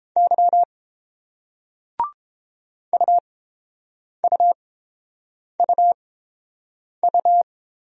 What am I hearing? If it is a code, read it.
Morse code, heard as a keyed tone, about 30 wpm, spaced as for 7 wpm, 700 Hz. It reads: Y UUUU